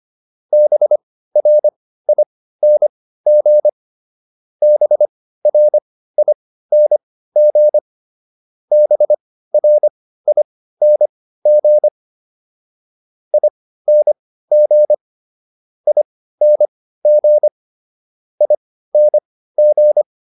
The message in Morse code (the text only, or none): BRING BRING BRING ING ING ING